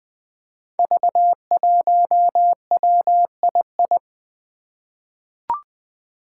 Morse code, heard as a keyed tone, about 20 wpm, 700 Hz. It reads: V1WII
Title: Five Challenging Letter-Number Combinations 20wpm